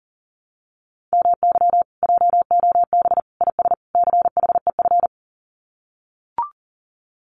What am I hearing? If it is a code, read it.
Morse code, heard as a keyed tone, about 40 wpm, 700 Hz. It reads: MY JOB IS CHEF
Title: Common QSO Phrases Encore 40wpm